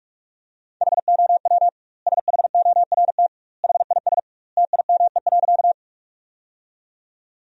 Morse code, heard as a keyed tone, about 46 wpm, 700 Hz. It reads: HOW SHORT HIS TIME.
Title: 2x Word-Spacing - Sentences from Top 400 Words - 45wpm